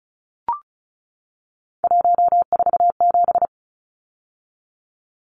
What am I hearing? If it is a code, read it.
Morse code, heard as a keyed tone, about 34 wpm, 700 Hz. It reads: 147
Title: Numbers - 3-Digit 35wpm